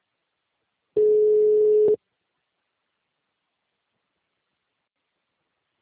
Ringback tone (Europe, 425 Hz).
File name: out- -2019- - - .6